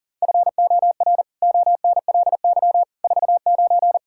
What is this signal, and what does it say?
Morse code, heard as a keyed tone, about 37 wpm, 700 Hz. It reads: FOR ONLY 49
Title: Pangram Challenge v2 - 40wpm